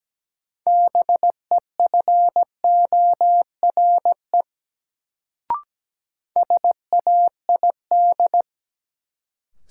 Morse code, heard as a keyed tone, about 17 wpm, 700 Hz. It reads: BEFORE SAID
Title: Sets of 2 Words - New Word Drill - 17wpm